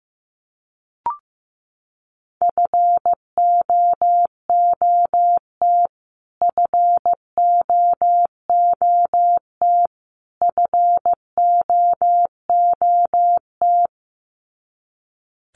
Morse code, heard as a keyed tone, about 15 wpm, 700 Hz. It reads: FOOT FOOT FOOT
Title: Top 500 Words Repeated 15wpm